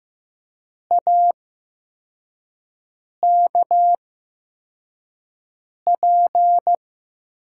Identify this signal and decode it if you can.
Morse code, heard as a keyed tone, about 15 wpm, spaced as for 4 wpm, 700 Hz. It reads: AKP
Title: USA Call Signs - 6 Characters - 8x Character Spacing - 15wpm